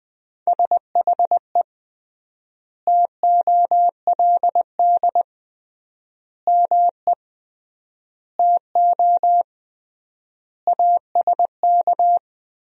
Morse code, heard as a keyed tone, about 20 wpm, 700 Hz. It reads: SHE TOLD ME TO ASK
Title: 3x Word-Spacing - Sentences from Top 400 Words - 20wpm